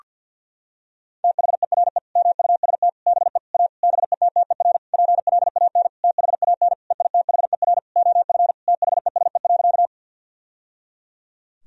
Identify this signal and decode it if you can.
Morse code, heard as a keyed tone, about 50 wpm, 700 Hz. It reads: THERE MUST BE A BETTER PLAN THAN EITHER OF THESE.
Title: Sentences from Top 1000 Words - Review Part 4 - 50wpm